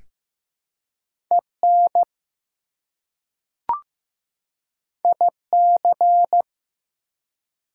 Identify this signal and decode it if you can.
Morse code, heard as a keyed tone, about 15 wpm, 700 Hz. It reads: EN IC
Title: Prefixes and Suffixes 15wpm